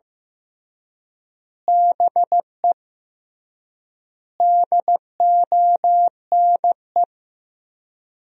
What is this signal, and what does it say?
Morse code, heard as a keyed tone, about 16 wpm, 700 Hz. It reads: BE DONE
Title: 3x Word-Spacing - Sentences from Top 500 Words - 15wpm